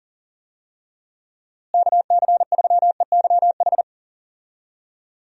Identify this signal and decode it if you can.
Morse code, heard as a keyed tone, about 40 wpm, 700 Hz. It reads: KC3EYH
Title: USA Call Signs - 6 Characters - Repeated 2x - 40wpm